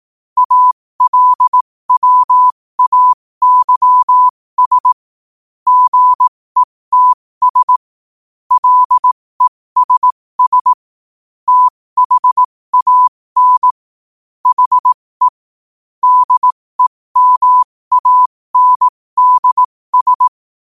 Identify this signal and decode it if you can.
Morse code, heard as a keyed tone, about 18 wpm, spaced as for 15 wpm, 1000 Hz. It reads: ALWAYS GETS LESS THAN HE DEMANDS
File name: qotdfiles / 15wpm